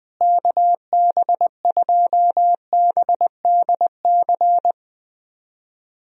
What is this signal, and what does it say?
Morse code, heard as a keyed tone, about 20 wpm, 700 Hz. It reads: KB2BDC
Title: USA Call Signs - 6 Characters - Repeated 2x - 20wpm